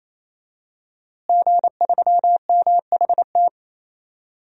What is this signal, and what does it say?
Morse code, heard as a keyed tone, about 27 wpm, 700 Hz. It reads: G3MHT
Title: DXCC-100 Call Signs - 5 Characters - Repeated 3x - 28wpm